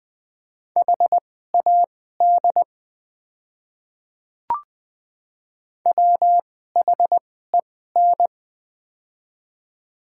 Morse code, heard as a keyed tone, about 20 wpm, spaced as for 14 wpm, 700 Hz. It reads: HAD WHEN
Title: 2x Character Spacing - Top 100 Words - 20wpm